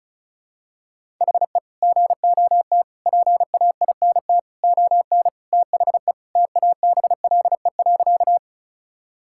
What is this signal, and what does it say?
Morse code, heard as a keyed tone, about 36 wpm, 700 Hz. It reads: HE GOT PAINT ON THE TABLE.
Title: Sentences from Top 1000 Words - Review Part 6 - 35wpm